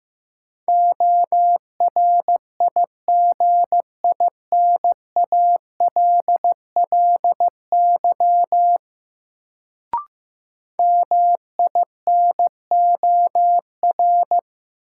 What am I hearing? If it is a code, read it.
Morse code, heard as a keyed tone, about 15 wpm, 700 Hz. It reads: ORIGINALLY MINOR
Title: Sentences from Top 1500 Words - New Word Drill Alt - 15wpm